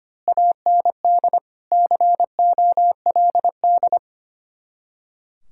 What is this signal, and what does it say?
Morse code, heard as a keyed tone, about 24 wpm, 700 Hz. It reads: AND COLD
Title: Speed-Racing Common QSO Phrases - 25wpm